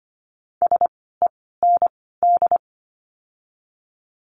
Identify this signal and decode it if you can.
Morse code, heard as a keyed tone, about 25 wpm, spaced as for 16 wpm, 700 Hz. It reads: SEND
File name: course / 25:10wpm / 094 Lesson 16f - Letter c - Words 25wpm